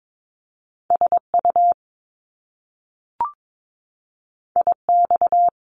Morse code, heard as a keyed tone, about 22 wpm, 700 Hz. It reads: SU IX
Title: Top Two Letter Combinations in Words 22wpm